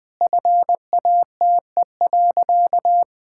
Morse code, heard as a keyed tone, about 20 wpm, 700 Hz. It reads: FATE.
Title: Sentences with 4-Word Phrases - 20wpm